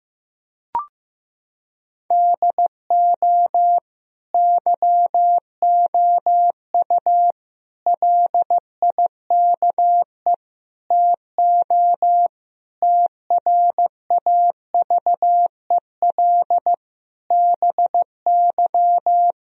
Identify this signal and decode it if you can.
Morse code, heard as a keyed tone, about 15 wpm, 700 Hz. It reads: DO YOU LIKE TO TRAVEL BY
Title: Sentences from Top 1000 Words - Review Part 3 - 15wpm